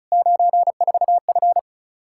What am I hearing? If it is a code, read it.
Morse code, heard as a keyed tone, about 34 wpm, 700 Hz. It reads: 94F
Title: Practice QSO - Exchange 2 - Info - 35wpm